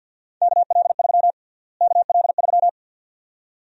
Morse code, heard as a keyed tone, about 50 wpm, 700 Hz. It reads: KL3 KL3
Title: USA Call Sign Prefixes - Repeated 2x - 50wpm